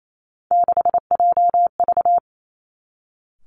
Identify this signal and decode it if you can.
Morse code, heard as a keyed tone, about 28 wpm, 700 Hz. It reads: 6JV
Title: Three Challenging Letter-Number Combinations 28wpm